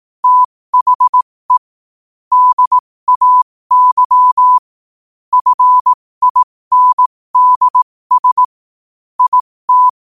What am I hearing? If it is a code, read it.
Morse code, heard as a keyed tone, about 18 wpm, spaced as for 15 wpm, 1000 Hz. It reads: THE DAY FINDS IT
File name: qotdfiles / 15wpm